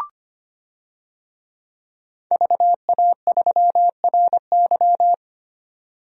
Morse code, heard as a keyed tone, about 25 wpm, 700 Hz. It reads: VA3RY
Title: DXCC-100 Call Signs - 5 Characters - 25wpm